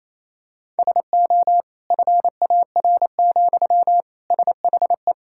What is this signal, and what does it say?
Morse code, heard as a keyed tone, about 28 wpm, 700 Hz. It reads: SO FAR, SHE
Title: Sentences from Top 1000 Words - Review Part 6 - 28wpm